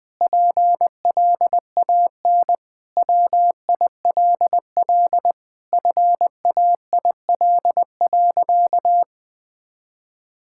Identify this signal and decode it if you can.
Morse code, heard as a keyed tone, about 20 wpm, 700 Hz. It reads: PLAN WILL FAIL.
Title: Sentences with 7-Word Phrases - Spoken Beginning - 20wpm